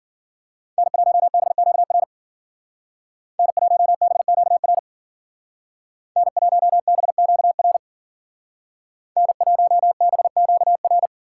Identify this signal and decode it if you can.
Morse code, heard as a keyed tone, about 49 wpm, 700 Hz. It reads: N1BQR N1BQR N1BQR N1BQR
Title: Speed Racing - USA Call Signs 40wpm